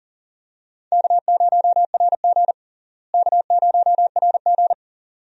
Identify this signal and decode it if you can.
Morse code, heard as a keyed tone, about 37 wpm, 700 Hz. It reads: K0RG K0RG
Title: USA Call Signs - 4 Characters - Repeated 2x - 40wpm